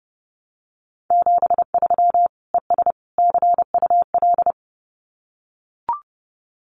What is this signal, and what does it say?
Morse code, heard as a keyed tone, about 30 wpm, 700 Hz. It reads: 73 ES CUL
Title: Common QSO Phrases Encore 30wpm